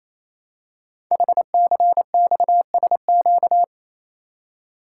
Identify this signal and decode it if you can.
Morse code, heard as a keyed tone, about 29 wpm, 700 Hz. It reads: HCXSQ
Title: Code Groups - 5 Letters - 28wpm